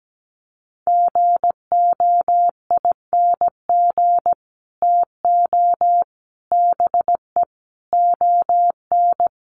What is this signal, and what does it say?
Morse code, heard as a keyed tone, about 17 wpm, 700 Hz. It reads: GOING TO BE ON
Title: Sets of 5 Words Encore2 17wpm